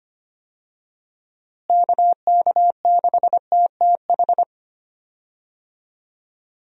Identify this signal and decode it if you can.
Morse code, heard as a keyed tone, about 25 wpm, 700 Hz. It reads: KK6TTH
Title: USA Call Signs - 6 Characters - Repeated 2x - 25wpm